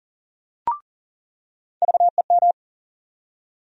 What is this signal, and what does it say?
Morse code, heard as a keyed tone, about 40 wpm, 700 Hz. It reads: VEM
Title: USA Call Sign Postfixes - 3 Letters 40wpm